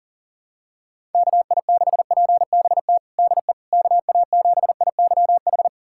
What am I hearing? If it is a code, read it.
Morse code, heard as a keyed tone, about 37 wpm, 700 Hz. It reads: KI6PBT DE KA7IYH
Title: USA Callsign DE Callsign - 6 Characters - 40wpm